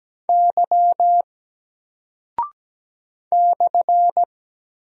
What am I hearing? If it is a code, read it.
Morse code, heard as a keyed tone, about 17 wpm, 700 Hz. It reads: Y /
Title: Single Character (Letter:Number:Punctuation:Prosign) Alt 17wpm